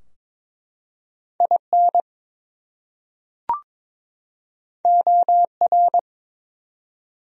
Morse code, heard as a keyed tone, about 22 wpm, 700 Hz. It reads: IN OR
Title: Top 100 Words 22wpm